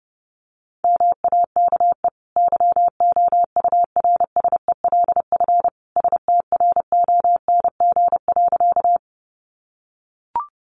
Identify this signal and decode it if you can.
Morse code, heard as a keyed tone, about 30 wpm, 700 Hz. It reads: MAKE YOURSELF STRONG.
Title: Sentences from Top 600 Words 30wpm